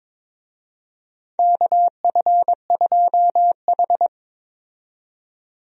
Morse code, heard as a keyed tone, about 22 wpm, 700 Hz. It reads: KF2H
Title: USA Call Signs - 4 Characters - 22wpm